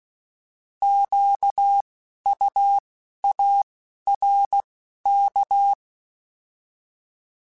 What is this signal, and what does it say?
Morse code, heard as a keyed tone, about 16 wpm, 800 Hz. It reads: Q U A R K
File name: Alphabetical / Alphabetical mp3s / Qs @10